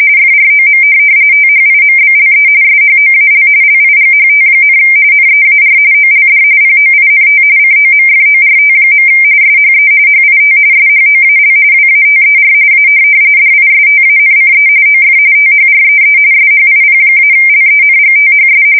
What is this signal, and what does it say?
RTTY, about 45.45 baud. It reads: FROM THIMBLE SHOAL LIGHT, NEAR OLD POINT COMFORT. SHE HIT SHOAL WATER A DISTANCE OF THREE SHIP-LENGTHS
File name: fsk170 / USS-Missouri